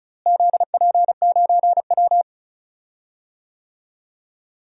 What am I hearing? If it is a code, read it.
Morse code, heard as a keyed tone, about 36 wpm, 700 Hz. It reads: ZP9W
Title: DXCC-100 Call Signs - 4 Characters - Repeated 2x - 35wpm